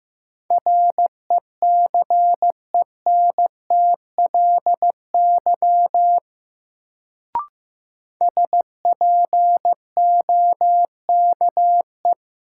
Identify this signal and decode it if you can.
Morse code, heard as a keyed tone, about 15 wpm, 700 Hz. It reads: RECENTLY SPOKE
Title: Sentences from Top 1200 Words - New Word Drill Alt - 15wpm